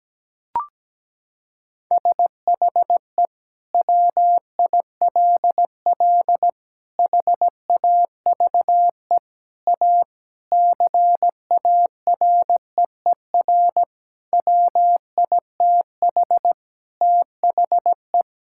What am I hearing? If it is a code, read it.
Morse code, heard as a keyed tone, about 17 wpm, 700 Hz. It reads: SHE WILL HAVE A CAREER WITH THE
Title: Sentences from Top 1000 Words - Review Part 6 - 17wpm